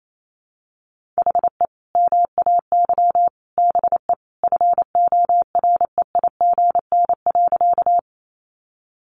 Morse code, heard as a keyed tone, about 28 wpm, 700 Hz. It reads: HE MAY BE FOREIGN.
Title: Sentences from Top 1400 Words 28wpm